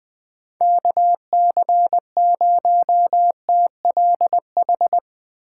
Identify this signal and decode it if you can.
Morse code, heard as a keyed tone, about 20 wpm, 700 Hz. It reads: KC0TLH
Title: USA Call Signs - 3x Character Spacing - 20wpm